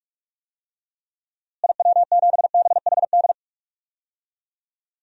Morse code, heard as a keyed tone, about 41 wpm, 700 Hz. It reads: IW7BHD
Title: DXCC-100 Call Signs - 6 Characters - Repeated 3x - 45wpm